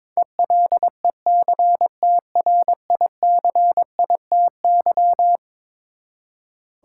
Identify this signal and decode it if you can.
Morse code, heard as a keyed tone, about 22 wpm, 700 Hz. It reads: ELECTRICITY
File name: practice / 22wpm / Eleven Letter Words - 22wpm